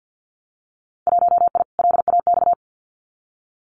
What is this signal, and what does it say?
Morse code, heard as a keyed tone, about 50 wpm, 700 Hz. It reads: 1I LAX
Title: Winter Field Day Exchanges Encore 50wpm